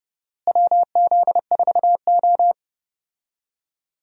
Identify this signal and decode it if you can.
Morse code, heard as a keyed tone, about 29 wpm, 700 Hz. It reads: WZ4O
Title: USA Call Signs - 4 Characters - Repeated 2x - 30wpm